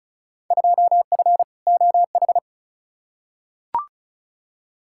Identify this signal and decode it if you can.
Morse code, heard as a keyed tone, about 36 wpm, 700 Hz. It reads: 2F OH
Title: Field Day Exchanges - 35wpm